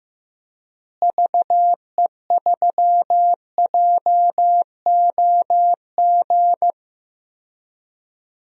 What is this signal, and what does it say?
Morse code, heard as a keyed tone, about 15 wpm, 700 Hz. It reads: VE3JOG
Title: DXCC-100 Call Signs - 6 Characters - Repeated 3x - 15wpm